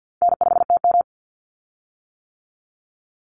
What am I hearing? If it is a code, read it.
Morse code, heard as a keyed tone, about 50 wpm, 700 Hz. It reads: N5TM
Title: Easy Call Signs - 50wpm